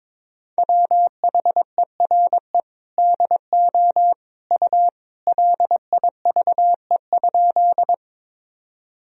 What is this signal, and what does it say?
Morse code, heard as a keyed tone, about 22 wpm, 700 Hz. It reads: WHERE DO U LIVE?
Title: Sentences from Top 200 Words and Abbreviations 22wpm